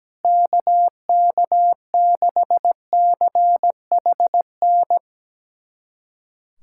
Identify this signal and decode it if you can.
Morse code, heard as a keyed tone, about 17 wpm, 700 Hz. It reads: KK6CHN
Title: USA Callsign DE Callsign 17wpm